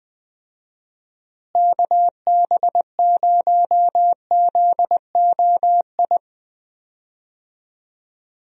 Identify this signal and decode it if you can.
Morse code, heard as a keyed tone, about 20 wpm, 700 Hz. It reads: KB0ZOI
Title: USA Call Signs - 6 Characters - Repeated 2x - 20wpm